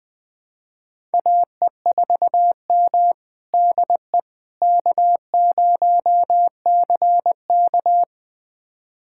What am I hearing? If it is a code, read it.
Morse code, heard as a keyed tone, about 20 wpm, 700 Hz. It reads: AE4M DE K0CK
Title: USA Callsign DE Callsign - 4 Characters - 20wpm